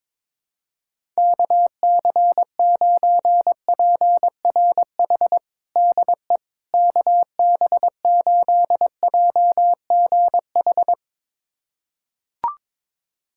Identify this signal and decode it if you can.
Morse code, heard as a keyed tone, about 21 wpm, 700 Hz. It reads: KC9PRH DE KB8JGH
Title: USA Callsign DE Callsign - 6 Characters - 22wpm